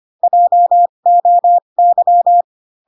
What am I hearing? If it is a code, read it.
Morse code, heard as a keyed tone, about 25 wpm, spaced as for 22 wpm, 700 Hz. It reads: JOY